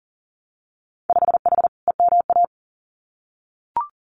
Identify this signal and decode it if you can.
Morse code, heard as a keyed tone, about 38 wpm, 700 Hz. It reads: 5H EMA
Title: Winter Field Day Exchanges Encore 40wpm